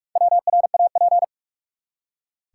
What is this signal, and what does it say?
Morse code, heard as a keyed tone, about 42 wpm, 700 Hz. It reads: WRAP
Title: Four Letter Words - 45wpm